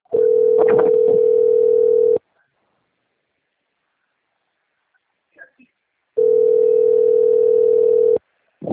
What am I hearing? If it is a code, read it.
Ringback tone (North America), 440 + 480 Hz.
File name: out- - - - - .74841